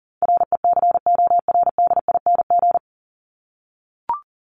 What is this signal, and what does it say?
Morse code, heard as a keyed tone, about 38 wpm, 700 Hz. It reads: RECORDING
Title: Top 1501-1750 Words 40wpm